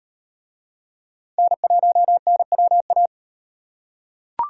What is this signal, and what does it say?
Morse code, heard as a keyed tone, about 37 wpm, 700 Hz. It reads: N1NWA